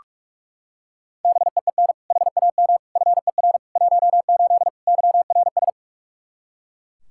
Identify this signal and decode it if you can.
Morse code, heard as a keyed tone, about 42 wpm, 700 Hz. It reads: BEEN HAM FER 18 YRS
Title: Practice QSO - Exchange 3 - More Info - 45wpm